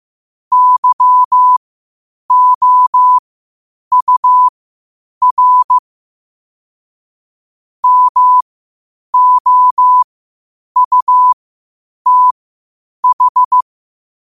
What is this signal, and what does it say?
Morse code, heard as a keyed tone, about 15 wpm, spaced as for 8 wpm, 1000 Hz. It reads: YOUR MOUTH
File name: qotdfiles / 8wpm